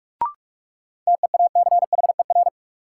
Morse code, heard as a keyed tone, about 42 wpm, 700 Hz. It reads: TEACHER
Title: Sentences from Top 900 Words - New Word Drill - 45wpm